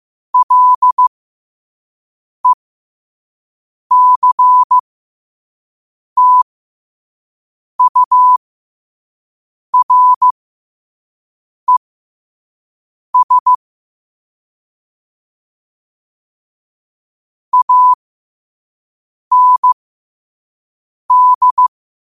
Morse code, heard as a keyed tone, about 15 wpm, spaced as for 5 wpm, 1000 Hz. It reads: LECTURES AND
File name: qotdfiles / 5wpm